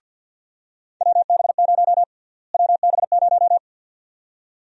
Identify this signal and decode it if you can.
Morse code, heard as a keyed tone, about 52 wpm, 700 Hz. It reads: WB0 WB0
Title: USA Call Sign Prefixes - Repeated 2x - 50wpm